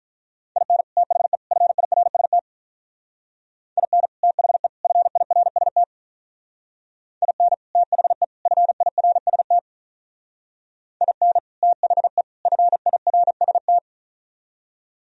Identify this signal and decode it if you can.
Morse code, heard as a keyed tone, about 46 wpm, 700 Hz. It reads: IN THE FIRST IN THE FIRST IN THE FIRST IN THE FIRST